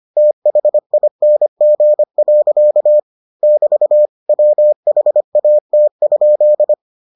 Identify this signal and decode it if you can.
Morse code, heard as a keyed tone, about 25 wpm, 600 Hz. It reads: THING. <BT> WHAT?